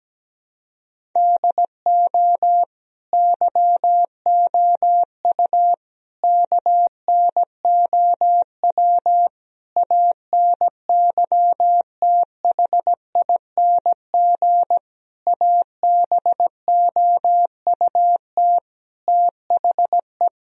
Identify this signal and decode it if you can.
Morse code, heard as a keyed tone, about 17 wpm, 700 Hz. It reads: DO YOU KNOW ANYTHING ABOUT THE
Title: Sentences from Top 1000 Words - Review Part 1 - 17wpm